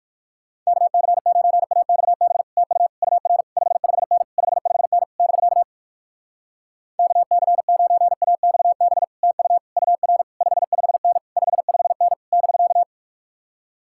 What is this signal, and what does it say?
Morse code, heard as a keyed tone, about 61 wpm, 700 Hz. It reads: KC9AXB TU UR 55N 55N <BK> KC9AXB TU UR 55N 55N <BK>